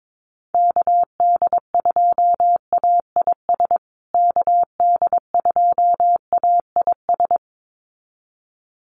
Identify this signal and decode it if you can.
Morse code, heard as a keyed tone, about 22 wpm, 700 Hz. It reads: KD2AIS KD2AIS
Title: USA Call Signs Repeated 2x 22wpm